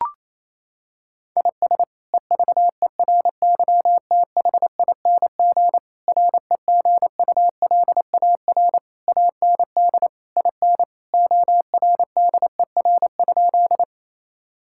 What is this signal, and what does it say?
Morse code, heard as a keyed tone, about 28 wpm, 700 Hz. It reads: IS EVERYTHING REGULAR AND IN ORDER?
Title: Sentences from Top 1000 Words - Review Part 6 - 28wpm